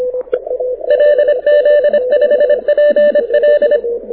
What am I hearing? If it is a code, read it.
Morse code, heard as a keyed tone, about 26 wpm, 590 Hz. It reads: LZ5PL
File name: audio zawody / IARU 2015 CW SN7Q